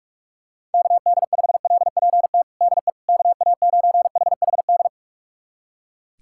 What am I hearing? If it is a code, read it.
Morse code, heard as a keyed tone, about 46 wpm, 700 Hz. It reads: KD5LPT DE KA9HHD
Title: USA Callsign DE Callsign 45wpm